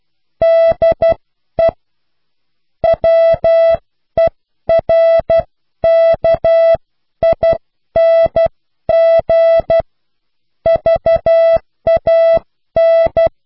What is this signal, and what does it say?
Morse code, heard as a keyed tone, about 12 wpm, spaced as for 10 wpm, 660 Hz. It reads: DE WERKING VAN